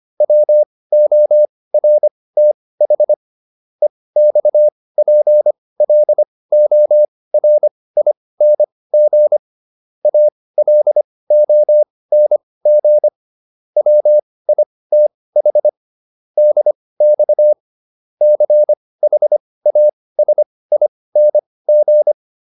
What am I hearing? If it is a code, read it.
Morse code, heard as a keyed tone, about 25 wpm, spaced as for 18 wpm, 600 Hz. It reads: WORTH EXPLORING ALONG WITH DX CHASING